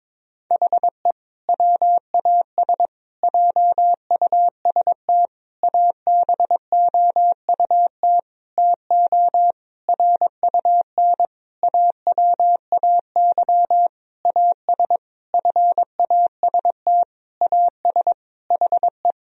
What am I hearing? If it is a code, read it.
Morse code, heard as a keyed tone, about 22 wpm, 700 Hz. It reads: HE WAS JUST ABOUT TO RUN AWAY AS FAST AS HE
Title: Sentences from Top 1000 Words - Review Part 2 - 22wpm